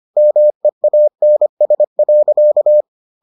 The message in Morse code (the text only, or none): MEANS.